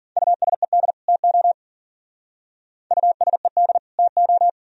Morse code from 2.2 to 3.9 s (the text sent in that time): USED